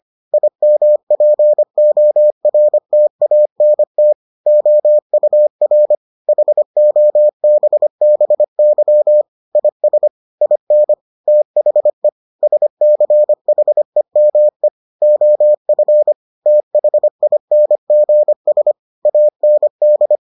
Morse code, heard as a keyed tone, about 25 wpm, 600 Hz. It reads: IMPORTANT OUR HOBBY IS IN THE SCHEME OF THINGS AND